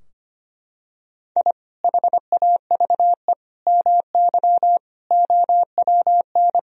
Morse code, heard as a keyed tone, about 25 wpm, 700 Hz. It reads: I HAVE MY OWN
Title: Sentences from Top 1000 Words - Review Part 3 - 25wpm